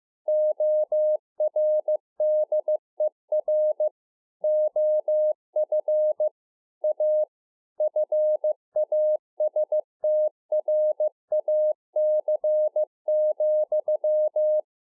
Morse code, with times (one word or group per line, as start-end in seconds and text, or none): 0.3-3.9 ORDER
4.4-6.3 OF
6.8-7.2 A
7.8-14.6 FASTRAC,